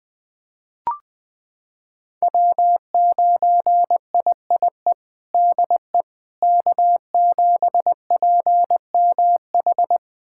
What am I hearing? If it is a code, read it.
Morse code, heard as a keyed tone, about 20 wpm, 700 Hz. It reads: W9IIE DE K7PMH
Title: USA Callsign DE Callsign - 5 Characters - 20wpm